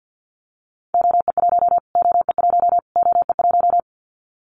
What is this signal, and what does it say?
Morse code, heard as a keyed tone, about 50 wpm, 700 Hz. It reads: OE1 OE1 OE1
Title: Callsign Prefixes for Top 25 DXCC - Repeated 3x - 50wpm